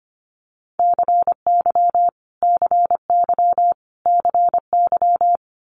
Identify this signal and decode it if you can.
Morse code, heard as a keyed tone, about 25 wpm, 700 Hz. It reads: CY CY CY
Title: Multi-letter Phonemes Repeated 3x 25wpm